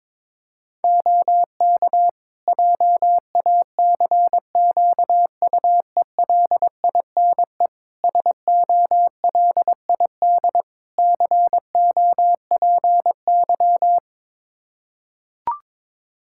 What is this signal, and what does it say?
Morse code, heard as a keyed tone, about 22 wpm, 700 Hz. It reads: OK JACQUELINE SOLID COPY
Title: K1USN SST QSO Practice 22wpm